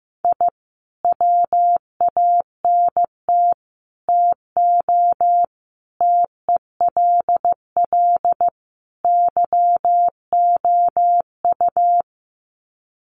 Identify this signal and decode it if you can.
Morse code, heard as a keyed tone, about 15 wpm, 700 Hz. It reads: I WANT TO TELL YOU
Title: Sets of 5 Words Encore 15wpm